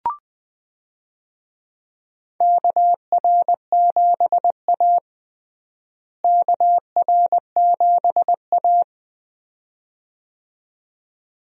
Morse code, heard as a keyed tone, about 20 wpm, 700 Hz. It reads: KR7A KR7A
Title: USA Call Signs - 4 Characters - Repeated 2x - 20wpm